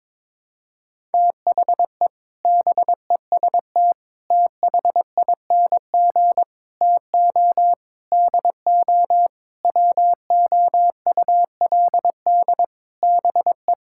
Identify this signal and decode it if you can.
Morse code, heard as a keyed tone, about 22 wpm, 700 Hz. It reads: THE BEST THING TO DO WOULD BE
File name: practice / 22wpm / Sets of 8 Words - 22wpm